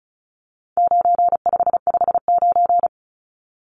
Morse code, heard as a keyed tone, about 34 wpm, 700 Hz. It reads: 9559
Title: Numbers - 4-Digit 35wpm